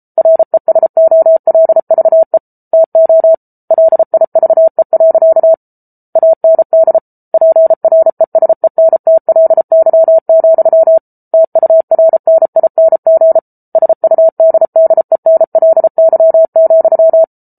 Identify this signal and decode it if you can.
Morse code, heard as a keyed tone, about 34 wpm, 650 Hz. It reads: RESOLVE TO LIVE. AND PRESENTLY, TURNING SUDDENLY,